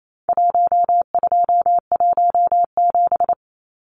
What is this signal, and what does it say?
Morse code, heard as a keyed tone, about 28 wpm, 700 Hz. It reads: 1217
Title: Numbers - Contest Style - 4-Digit 28wpm